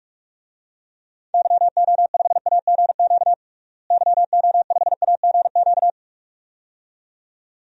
Morse code, heard as a keyed tone, about 46 wpm, 690 Hz. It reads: YO5AGQ YO5AGQ
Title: DXCC-100 Call Signs - 6 Characters - Repeated 2x - 45wpm